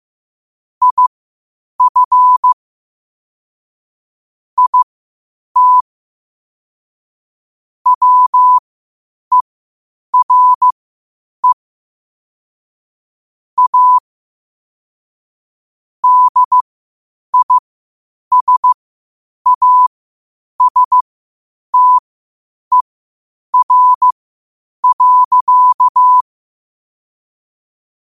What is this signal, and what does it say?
Morse code, heard as a keyed tone, about 15 wpm, spaced as for 8 wpm, 1000 Hz. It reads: IF IT WERE A DISASTER.
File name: qotdfiles / 8wpm